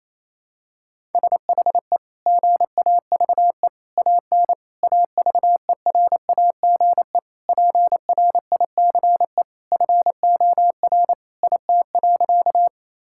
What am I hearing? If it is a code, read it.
Morse code, heard as a keyed tone, about 28 wpm, 700 Hz. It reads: SHE GAVE AN AVERAGE PRICE FOR IT.
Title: Sentences from Top 1000 Words - Review Part 6 - 28wpm